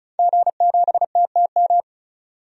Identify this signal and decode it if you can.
Morse code, heard as a keyed tone, about 36 wpm, 700 Hz. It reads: G7TTM
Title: DXCC-100 Call Signs - 5 Characters - 35wpm